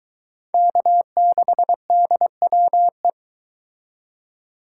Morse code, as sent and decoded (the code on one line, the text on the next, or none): -.- -.... -.. .-- .
K6DWE